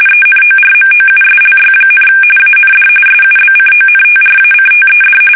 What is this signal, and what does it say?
RTTY, about 45.45 baud. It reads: ENORMOUSLY, AND SUBCONTRACTED
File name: fsk850